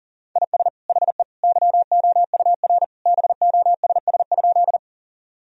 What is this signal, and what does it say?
Morse code, heard as a keyed tone, about 40 wpm, 700 Hz. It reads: IS HE YOUR BOSS?
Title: Sentences from Top 1000 Words - Review Part 4 - 40wpm